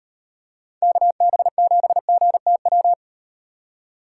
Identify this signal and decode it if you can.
Morse code, heard as a keyed tone, about 38 wpm, 700 Hz. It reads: KB7GTW